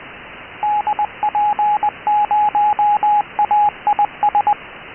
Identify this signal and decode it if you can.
Morse code, heard as a keyed tone, about 20 wpm, 830 Hz, with radio noise. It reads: DP0AIS